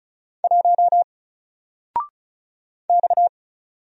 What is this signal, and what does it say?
Morse code, heard as a keyed tone, about 36 wpm, 700 Hz. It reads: 1 X
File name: Single-Letter-Number-35wpm